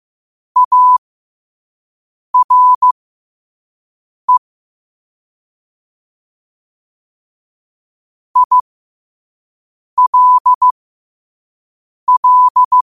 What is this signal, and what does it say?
Morse code, heard as a keyed tone, about 15 wpm, spaced as for 5 wpm, 1000 Hz. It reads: ARE ILL
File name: qotdfiles / 5wpm